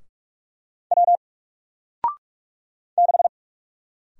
Morse code, heard as a keyed tone, about 44 wpm, 700 Hz. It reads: W 6
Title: Single Letter-Number Alt 45wpm